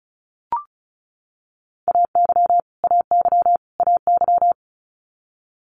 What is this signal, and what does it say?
Morse code, heard as a keyed tone, about 36 wpm, 700 Hz. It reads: AY AY AY
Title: Multi-letter Phonemes Repeated 3x 35wpm